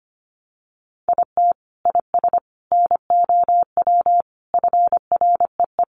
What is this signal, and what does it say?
Morse code, heard as a keyed tone, about 25 wpm, 700 Hz. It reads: IT IS NOW FREE
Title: Sentences from Top 1300 Words 25wpm